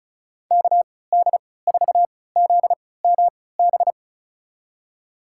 Morse code, heard as a keyed tone, about 36 wpm, spaced as for 20 wpm, 700 Hz. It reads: KD4ZMB
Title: USA Call Signs - 3x Character Spacing - 35wpm